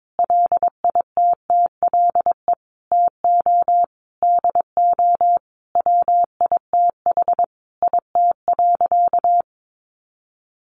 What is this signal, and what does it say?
Morse code, heard as a keyed tone, about 22 wpm, 700 Hz. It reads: LITTLE TO DO WITH IT.
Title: Sentences from Top 1500 Words 22wpm